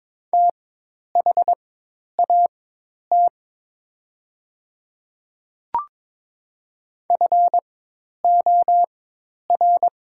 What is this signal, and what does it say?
Morse code, heard as a keyed tone, about 22 wpm, spaced as for 10 wpm, 700 Hz. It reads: THAT FOR
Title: 4x Character Spacing - Top 100 Words - 22wpm